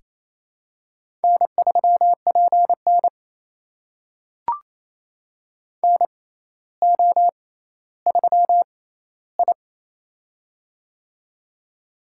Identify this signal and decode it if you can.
Morse code, heard as a keyed tone, about 28 wpm, spaced as for 14 wpm, 700 Hz. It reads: N3PN NO3I
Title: USA Call Signs - 4 Characters - 6x Character Spacing - 28wpm